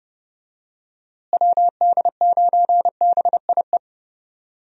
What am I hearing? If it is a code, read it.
Morse code, heard as a keyed tone, about 30 wpm, 700 Hz. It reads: WD9BIE
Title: USA Call Signs - 6 Characters - Repeated 3x - 30wpm